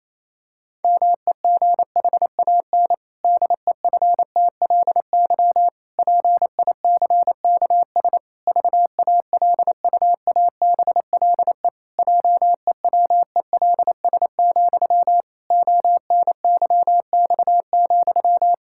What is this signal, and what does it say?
Morse code, heard as a keyed tone, about 28 wpm, 700 Hz. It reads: MEGHAN DEFTLY PICKS VALUABLE JEWELS, ONYX,
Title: Pangram Challenge v2 - 28wpm